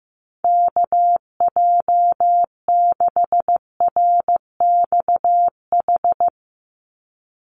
Morse code, heard as a keyed tone, about 15 wpm, 700 Hz. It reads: KJ6RXH
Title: USA Call Signs 15wpm